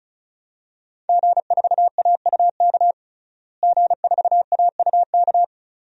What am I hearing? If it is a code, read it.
Morse code, heard as a keyed tone, about 34 wpm, 700 Hz. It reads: G4AUK G4AUK
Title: DXCC-100 Call Signs - 5 Characters - Repeated 3x - 35wpm